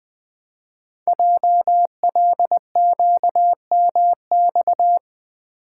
Morse code, heard as a keyed tone, about 20 wpm, 700 Hz. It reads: JLQMX
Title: Code Groups - 5 Letters - 20wpm